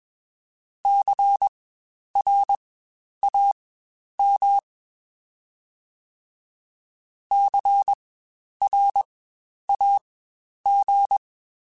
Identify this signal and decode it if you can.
Morse code, heard as a keyed tone, about 21 wpm, spaced as for 10 wpm, 800 Hz. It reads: CRAM CRAG
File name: Spaced Alphas / Spaced Alphas mp3s / Cs sp @20